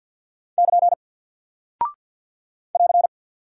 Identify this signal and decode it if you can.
Morse code, heard as a keyed tone, about 50 wpm, 700 Hz. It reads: <KN> <AR>
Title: Single Character (Letter:Number:Punctuation:Prosign) Alt 50wpm